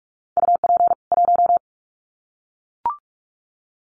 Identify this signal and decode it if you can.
Morse code, heard as a keyed tone, about 44 wpm, 700 Hz. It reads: UP 1
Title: Common QSO Phrases 45wpm